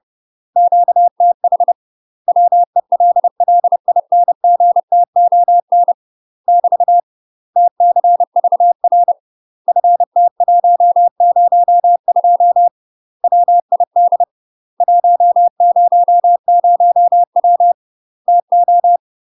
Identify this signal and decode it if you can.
Morse code, heard as a keyed tone, about 30 wpm, 700 Hz. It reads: QTH WELLINGTON <BT> TCVR FT102 WID 100W TO
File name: UK MASTER QSOs TRAINING / 30WPM / UK MASTER QSOs 0001